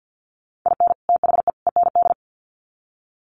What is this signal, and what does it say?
Morse code, heard as a keyed tone, about 50 wpm, 700 Hz. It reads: IN THE END
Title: Sets of 3 Words 50wpm